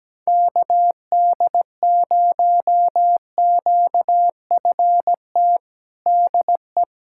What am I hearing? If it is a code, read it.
Morse code, heard as a keyed tone, about 17 wpm, 700 Hz. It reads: KD0QFT DE
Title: USA Callsign DE Callsign - 6 Characters - 17wpm